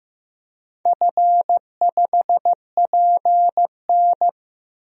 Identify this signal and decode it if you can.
Morse code, heard as a keyed tone, about 15 wpm, 700 Hz. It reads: F5PN
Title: DXCC-100 Call Signs - 4 Characters - Repeated 2x - 15wpm